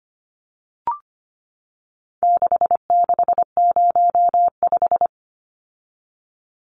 Morse code, heard as a keyed tone, about 25 wpm, 700 Hz. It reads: 6605
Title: Numbers - 4-Digit 25wpm